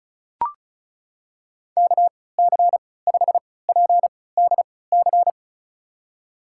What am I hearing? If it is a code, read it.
Morse code, heard as a keyed tone, about 36 wpm, spaced as for 20 wpm, 700 Hz. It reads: KC5PDC
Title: USA Call Signs - 6 Characters - 3x Character Spacing - 35wpm